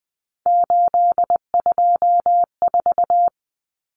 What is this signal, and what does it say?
Morse code, heard as a keyed tone, about 20 wpm, 700 Hz. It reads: 824
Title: Numbers - 3-Digit 20wpm